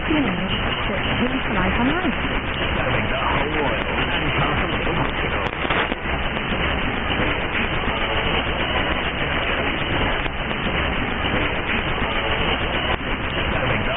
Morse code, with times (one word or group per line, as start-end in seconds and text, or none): none